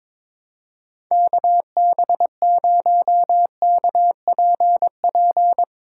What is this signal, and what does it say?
Morse code, heard as a keyed tone, about 22 wpm, 700 Hz. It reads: KB0KPP
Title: USA Callsign DE Callsign 22wpm